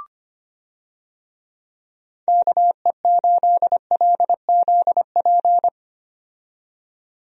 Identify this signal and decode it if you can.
Morse code, heard as a keyed tone, about 25 wpm, 700 Hz. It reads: KE8LZP
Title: USA Call Signs - 6 Characters - 25wpm